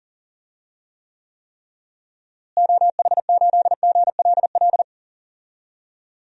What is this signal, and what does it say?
Morse code, heard as a keyed tone, about 40 wpm, 700 Hz. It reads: OH8GLL
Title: DXCC-100 Call Signs - 6 Characters - Repeated 3x - 40wpm